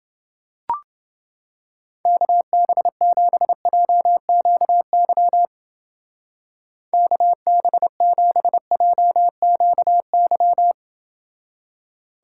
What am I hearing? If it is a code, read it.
Morse code, heard as a keyed tone, about 30 wpm, 700 Hz. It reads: KB7JQY KB7JQY